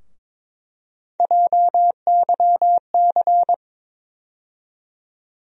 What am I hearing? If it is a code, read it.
Morse code, heard as a keyed tone, about 22 wpm, 700 Hz. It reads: JYC
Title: USA Call Sign Postfixes - 3 Letters 22wpm